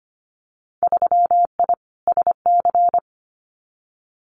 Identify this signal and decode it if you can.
Morse code, heard as a keyed tone, about 25 wpm, 700 Hz. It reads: 3I SC
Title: Winter Field Day Exchanges Encore 25wpm